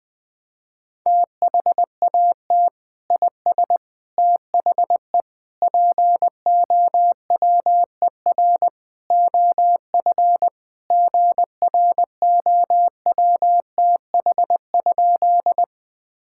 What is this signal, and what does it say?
Morse code, heard as a keyed tone, about 20 wpm, 700 Hz. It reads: THAT IS THE POWER OF GROWTH?
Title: Sentences from Top 1000 Words - Review Part 5 - 20wpm